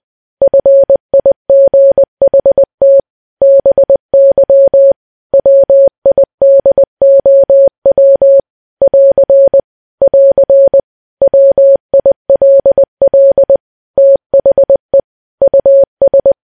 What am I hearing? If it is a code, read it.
Morse code, heard as a keyed tone, about 20 wpm, 550 Hz. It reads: FIGHT BY WIDOW <AR> <AR> WILL THE US
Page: news-headlines-in-morse-code-at-20-wpm